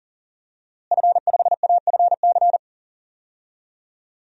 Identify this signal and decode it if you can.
Morse code, heard as a keyed tone, about 40 wpm, 700 Hz. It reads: F5AFC
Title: DXCC-100 Call Signs - 5 Characters - Repeated 3x - 40wpm